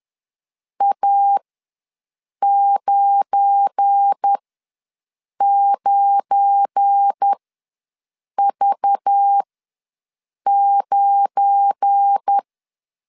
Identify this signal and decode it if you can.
Morse code, heard as a keyed tone, about 10 wpm, spaced as for 6 wpm, 800 Hz. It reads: A99V9
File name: LEZIONI / 7 GF9Z / 7 GF9 ws 10wpm 4034